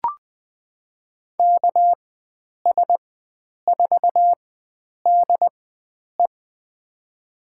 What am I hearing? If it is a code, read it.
Morse code, heard as a keyed tone, about 20 wpm, spaced as for 9 wpm, 700 Hz. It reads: KS4DE
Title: USA Call Signs - 4x Character Spacing - 20wpm